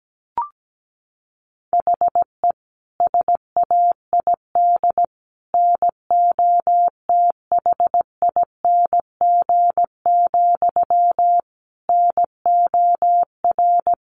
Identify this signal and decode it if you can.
Morse code, heard as a keyed tone, about 17 wpm, 700 Hz. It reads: HE SAID NOTHING, NOR
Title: Sentences from Top 1500 Words 17wpm